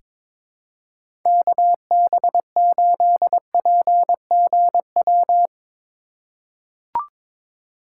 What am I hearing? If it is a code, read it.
Morse code, heard as a keyed tone, about 22 wpm, 700 Hz. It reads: KB8PGW
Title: K1USN SST QSO Practice - 22wpm